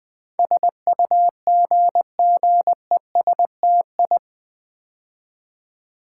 Morse code, heard as a keyed tone, about 20 wpm, 700 Hz. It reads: SUGGESTI
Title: Build A Word - 4 Letters At A Time - 10-Letter Words - 20wpm